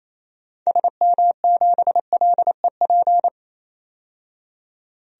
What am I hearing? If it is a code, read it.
Morse code, heard as a keyed tone, about 27 wpm, 700 Hz. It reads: SM7LEP
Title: DXCC-100 Call Signs - 6 Characters - 28wpm